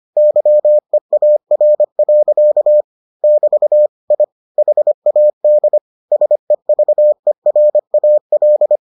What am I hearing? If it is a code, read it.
Morse code, heard as a keyed tone, about 25 wpm, 600 Hz. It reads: YEAR. <BT> I HAD SEVERAL